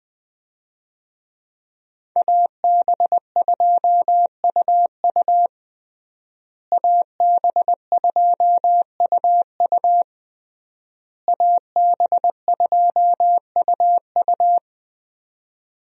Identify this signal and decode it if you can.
Morse code, heard as a keyed tone, about 20 wpm, 700 Hz. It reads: AB2UU AB2UU AB2UU
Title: USA Call Signs - 5 Characters - Repeated 3x - 20wpm